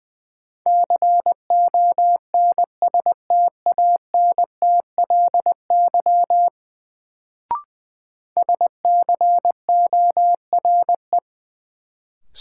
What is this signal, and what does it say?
Morse code, heard as a keyed tone, about 20 wpm, 700 Hz. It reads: CONSTANTLY SCORE
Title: Sentences from Top 1600 Words - New Word Drill Alt - 20wpm